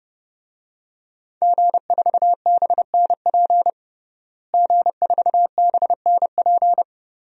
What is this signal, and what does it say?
Morse code, heard as a keyed tone, about 32 wpm, 700 Hz. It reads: G4BNP G4BNP
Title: DXCC-100 Call Signs - 5 Characters - Repeated 3x - 30wpm